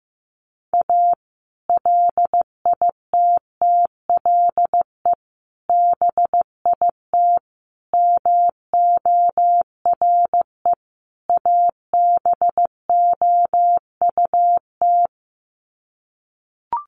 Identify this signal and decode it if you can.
Morse code, heard as a keyed tone, about 15 wpm, 700 Hz. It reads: A LITTLE BIT MORE ABOUT
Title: Sets of 5 Words Encore2 15wpm